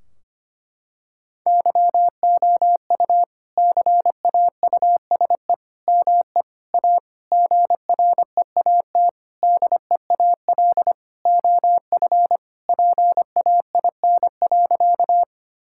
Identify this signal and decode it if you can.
Morse code, heard as a keyed tone, about 25 wpm, 700 Hz. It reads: YOU CAUSE ME A GREAT DEAL OF PAIN.
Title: Sentences from Top 1600 Words 25wpm